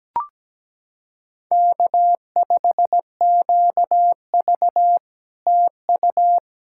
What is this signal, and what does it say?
Morse code, heard as a keyed tone, about 18 wpm, 700 Hz. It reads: K5QV TU
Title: SOTA QSO Practice 17wpm